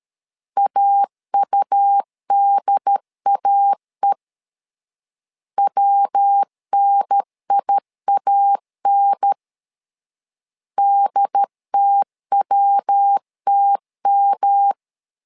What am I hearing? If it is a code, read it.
Morse code, heard as a keyed tone, about 13 wpm, 800 Hz. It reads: AUDAE WNIAN DTWTM
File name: LEZIONI / 3 MWRO / 3 MW 12wpm 1633